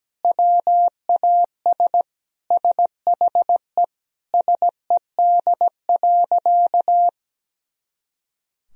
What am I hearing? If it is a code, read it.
Morse code, heard as a keyed tone, about 17 wpm, 700 Hz. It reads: WAS SHE SED.
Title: Sentences from Top 400 Words and Abbreviations 17wpm